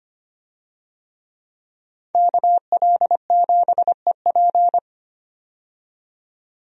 Morse code, heard as a keyed tone, about 25 wpm, 700 Hz. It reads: KL7EP
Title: DXCC-100 Call Signs - 5 Characters - 25wpm